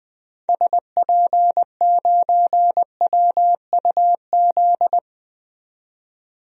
Morse code, heard as a keyed tone, about 20 wpm, 700 Hz. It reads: SP9WUZ
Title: DXCC-100 Call Signs - 6 Characters - 20wpm